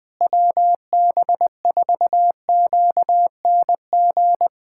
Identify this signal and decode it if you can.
Morse code, heard as a keyed tone, about 20 wpm, 700 Hz. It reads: WB4QNG
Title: USA Callsign DE Callsign - 6 Characters - 20wpm